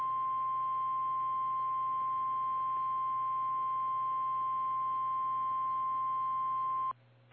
Emergency Alert System (SAME) tones (the 1050 Hz attention tone). It is heard from 1.1 to 6.5 s.
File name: eas / KKPRKXPN / KKPRKXPN